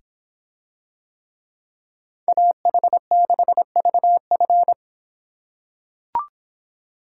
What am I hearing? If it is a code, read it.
Morse code, heard as a keyed tone, about 26 wpm, 700 Hz. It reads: AH6VF